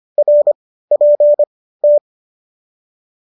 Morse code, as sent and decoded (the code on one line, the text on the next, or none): .-. .--. -
RPT